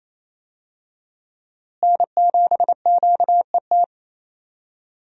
Morse code, heard as a keyed tone, about 28 wpm, 700 Hz. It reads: N7QET
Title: USA Call Signs - 5 Characters - Repeated 2x - 28wpm